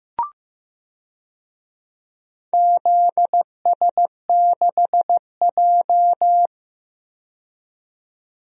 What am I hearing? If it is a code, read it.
Morse code, heard as a keyed tone, about 15 wpm, 700 Hz. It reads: ZS6J
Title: DXCC-100 Call Signs - 4 Characters - 15wpm